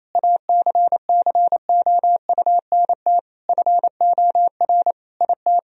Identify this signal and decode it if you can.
Morse code, heard as a keyed tone, about 27 wpm, 700 Hz. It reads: ACCOUNT FOR IT
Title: Sentences from Top 1000 Words - Review Part 3 - 28wpm